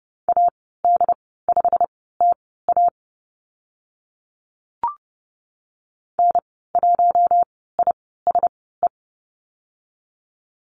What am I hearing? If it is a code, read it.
Morse code, heard as a keyed tone, about 30 wpm, spaced as for 17 wpm, 700 Hz. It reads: AD5TA N1ISE
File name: Lesson 15g - Number 5 - Call Signs 30wpm